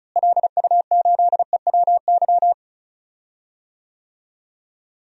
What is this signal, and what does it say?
Morse code, heard as a keyed tone, about 36 wpm, 700 Hz. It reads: LU8EWY
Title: DXCC-100 Call Signs - 6 Characters - Repeated 2x - 35wpm